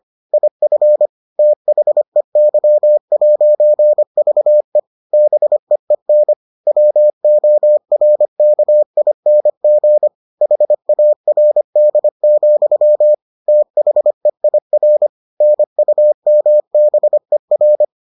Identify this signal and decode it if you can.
Morse code, heard as a keyed tone, about 25 wpm, 600 Hz. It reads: IF THEY'VE BEEN WORKING HARD, THEIR NUMBER